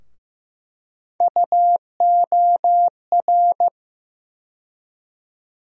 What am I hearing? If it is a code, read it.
Morse code, heard as a keyed tone, about 15 wpm, 700 Hz. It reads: UOR
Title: USA Call Sign Postfixes - 3 Letters 15wpm